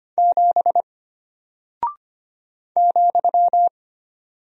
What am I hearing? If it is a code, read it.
Morse code, heard as a keyed tone, about 25 wpm, 700 Hz. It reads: 7 ,
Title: Single Character (Letter:Number:Punctuation:Prosign) Alt 25wpm